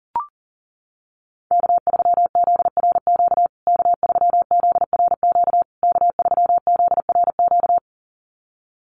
Morse code, heard as a keyed tone, about 40 wpm, 700 Hz. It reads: K3ZRQ K3ZRQ K3ZRQ
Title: USA Call Signs Repeated 3x 40wpm